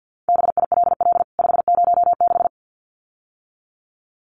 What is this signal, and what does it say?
Morse code, heard as a keyed tone, about 48 wpm, 700 Hz. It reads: BILL 506
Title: CWT Exchange 50wpm